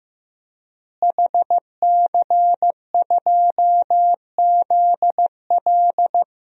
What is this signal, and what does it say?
Morse code, heard as a keyed tone, about 15 wpm, 700 Hz. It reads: HC2ZL
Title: DXCC-100 Call Signs - 5 Characters - Repeated 2x - 15wpm